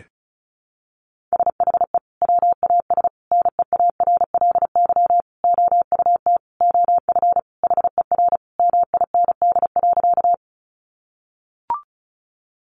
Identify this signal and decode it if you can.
Morse code, heard as a keyed tone, about 34 wpm, 700 Hz. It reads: SHE WAS NEARLY OUT OF HER MIND.
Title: Sentences from Top 1300 Words 35wpm